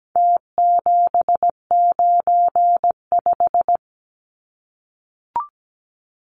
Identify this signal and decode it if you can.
Morse code, heard as a keyed tone, about 17 wpm, 700 Hz. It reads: T795
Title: Numbers - Contest Style - 4-Digit 17wpm